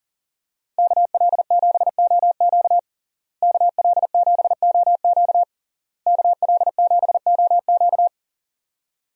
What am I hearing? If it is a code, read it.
Morse code, heard as a keyed tone, about 40 wpm, 700 Hz. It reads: KL7OQ KL7OQ KL7OQ
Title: DXCC-100 Call Signs - 5 Characters - Repeated 3x - 40wpm